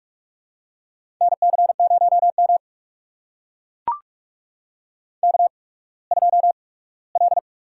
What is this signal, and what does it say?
Morse code, heard as a keyed tone, about 44 wpm, spaced as for 21 wpm, 690 Hz. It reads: NC0M K2L
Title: USA Call Signs - 4 Characters - 8x Character Spacing - 45wpm